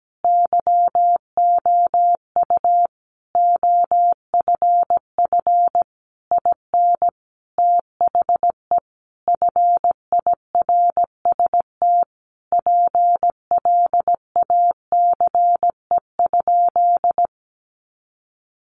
Morse code, heard as a keyed tone, about 17 wpm, 700 Hz. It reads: YOU OFF IN THE FIRST PLACE?
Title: Sentences from Top 200 Words 17wpm